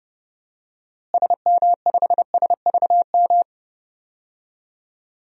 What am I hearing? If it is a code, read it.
Morse code, heard as a keyed tone, about 30 wpm, 700 Hz. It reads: SM5SVM
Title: DXCC-100 Call Signs - 6 Characters - Repeated 2x - 30wpm